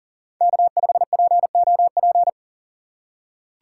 Morse code, heard as a keyed tone, about 40 wpm, 700 Hz. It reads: K5POP
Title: USA Call Signs - 2x Character Spacing - 40wpm